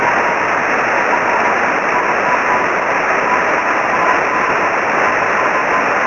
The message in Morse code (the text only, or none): none